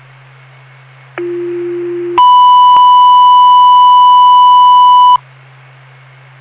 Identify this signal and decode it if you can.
Two-tone page, heard as a one-second tone, then a three-second tone, about 350 then 990 Hz. It is heard from 1.2 to 5.2 s.
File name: Frankford FD 2026 05 14 10 57 50